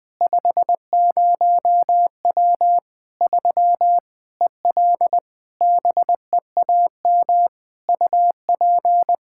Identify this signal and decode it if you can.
Morse code, heard as a keyed tone, about 20 wpm, 700 Hz. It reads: 50W 3 EL BEAM UP
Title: K1USN SST QSO Practice 20wpm